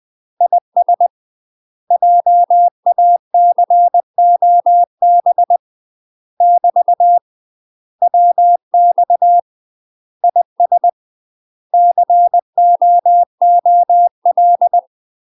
Morse code, heard as a keyed tone, about 20 wpm, 700 Hz. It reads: IS JACOB <BT> WX IS COOL
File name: UK MASTER QSOs TRAINING / 20WPM / UK MASTER QSOs 0044